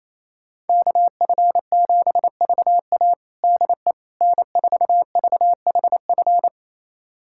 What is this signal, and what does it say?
Morse code, heard as a keyed tone, about 28 wpm, 700 Hz. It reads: KF7VA DE N4VHF
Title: USA Callsign DE Callsign - 5 Characters - 28wpm